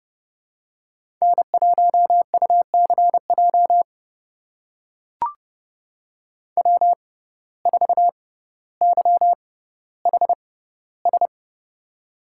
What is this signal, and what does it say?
Morse code, heard as a keyed tone, about 29 wpm, spaced as for 15 wpm, 700 Hz. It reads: N1UCJ W4YHS
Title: USA Call Signs - 5 Characters - 6x Character Spacing - 30wpm